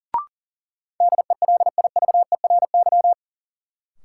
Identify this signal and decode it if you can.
Morse code, heard as a keyed tone, about 38 wpm, 700 Hz. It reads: DELIVERY
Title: Sentences from Top 1500 Words - New Word Drill - 40wpm